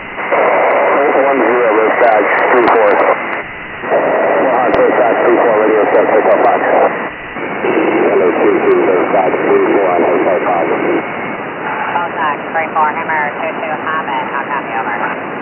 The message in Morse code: none